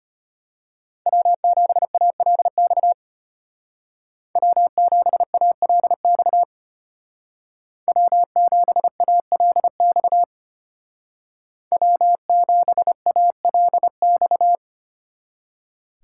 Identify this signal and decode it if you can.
Morse code, heard as a keyed tone, about 33 wpm, 700 Hz. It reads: W7ALX W7ALX W7ALX W7ALX